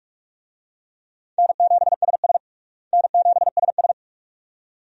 Morse code, heard as a keyed tone, about 46 wpm, 700 Hz. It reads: N7SS N7SS
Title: USA Call Signs - 4 Characters - Repeated 2x - 45wpm